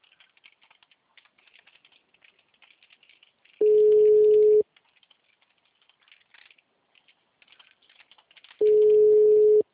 Ringback tone (Europe, 425 Hz).